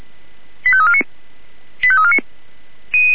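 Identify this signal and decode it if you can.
Five-tone selective call (ZVEI), heard as quick runs of five tones, tones of about 70 ms. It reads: 85349 85349
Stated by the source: Five-tone ZVEI selective call received on SDR.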